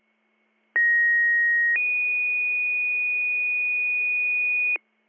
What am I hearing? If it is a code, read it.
Two-tone page, heard as a one-second tone, then a three-second tone, about 1900 then 2450 Hz, with noise. It is heard from 0.7 to 4.8 s.